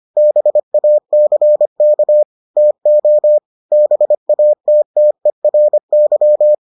Morse code, heard as a keyed tone, about 25 wpm, 600 Hz. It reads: BACK TO BATTERY